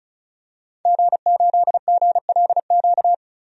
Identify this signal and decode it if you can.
Morse code, heard as a keyed tone, about 36 wpm, 700 Hz. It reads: G8GLQ
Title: DXCC-100 Call Signs - 5 Characters - 35wpm